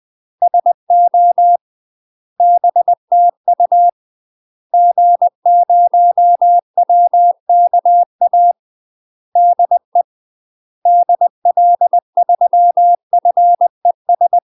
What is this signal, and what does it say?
Morse code, heard as a keyed tone, about 20 wpm, 700 Hz. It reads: SO BTU G0WKA DE DL3FES
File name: UK MASTER QSOs TRAINING / 20WPM / UK MASTER QSOs 0026